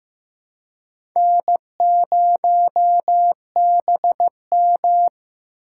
Morse code, heard as a keyed tone, about 15 wpm, 700 Hz. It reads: N0BM
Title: USA Call Signs - 4 Characters - 4x Character Spacing - 15wpm